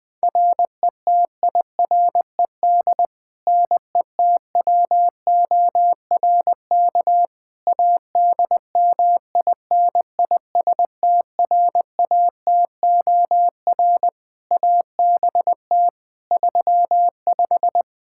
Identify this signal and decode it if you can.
Morse code, heard as a keyed tone, about 20 wpm, 700 Hz. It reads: RETIRED NETWORK ADMINISTRATOR ABT 35
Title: Practice QSO - Exchange 3 - More Info - 20wpm